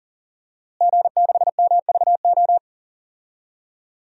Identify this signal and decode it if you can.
Morse code, heard as a keyed tone, about 37 wpm, 700 Hz. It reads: G6MVO
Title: DXCC-100 Call Signs - 5 Characters - Repeated 2x - 40wpm